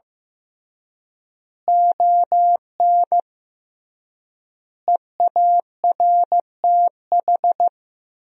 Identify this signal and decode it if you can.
Morse code, heard as a keyed tone, about 14 wpm, 700 Hz. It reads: ON EARTH
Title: 3x Word-Spacing - Sentences from Top 200 Words - 15wpm